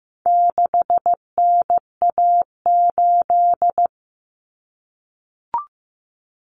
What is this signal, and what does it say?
Morse code, heard as a keyed tone, about 15 wpm, 700 Hz. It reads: 6NA8
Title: Common Cut Numbers - 4-Digit 15wpm